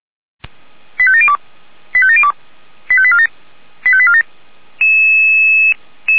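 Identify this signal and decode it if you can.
Five-tone selective call (ZVEI), heard as quick runs of five tones, tones of about 70 ms. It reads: 85702 85702 85847 85847
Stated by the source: ZVEI five-tone selective call received on SDR.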